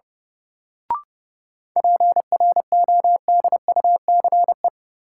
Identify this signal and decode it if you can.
Morse code, heard as a keyed tone, about 31 wpm, 700 Hz. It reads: PRODUCE
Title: Sentences from Top 1000 Words - Review Part 3 - New Word Drill - 30wpm